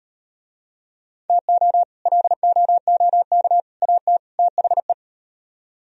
Morse code, heard as a keyed tone, about 36 wpm, 690 Hz. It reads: TO LOOK AT THE